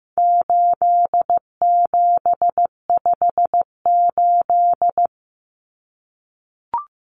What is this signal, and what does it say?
Morse code, heard as a keyed tone, about 15 wpm, 700 Hz. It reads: 8758
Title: Numbers - 4-Digit 15wpm